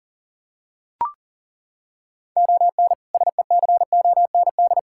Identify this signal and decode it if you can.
Morse code, heard as a keyed tone, about 40 wpm, 700 Hz. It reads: ON SECOND
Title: Sentences from Top 1600 Words 40wpm